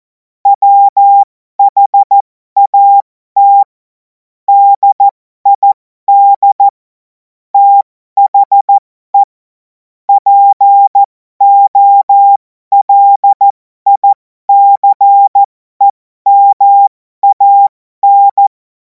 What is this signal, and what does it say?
Morse code, heard as a keyed tone, about 14 wpm, spaced as for 12 wpm, 800 Hz. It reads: WHAT DID THE POLICEMAN
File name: Morsekurs / Witze